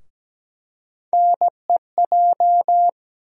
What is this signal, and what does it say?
Morse code, heard as a keyed tone, about 17 wpm, 700 Hz. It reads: NEJ
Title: USA Call Sign Postfixes - 3 Letters 17wpm